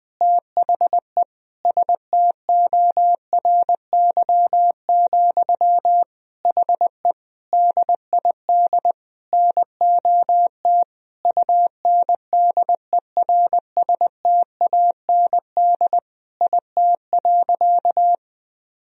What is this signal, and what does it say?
Morse code, heard as a keyed tone, about 21 wpm, 700 Hz. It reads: THE STORY, HE DID NOT UNDERSTAND IT.
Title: Sentences from Top 1000 Words - Review Part 3 - 20wpm